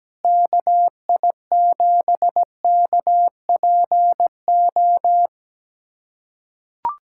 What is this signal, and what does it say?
Morse code, heard as a keyed tone, about 17 wpm, 700 Hz. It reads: KI7KPO
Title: USA Callsign DE Callsign - 6 Characters - 17wpm